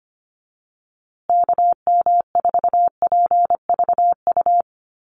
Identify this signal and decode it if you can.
Morse code, heard as a keyed tone, about 25 wpm, 700 Hz. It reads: KM4PVU
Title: USA Call Signs Repeated 3x 25wpm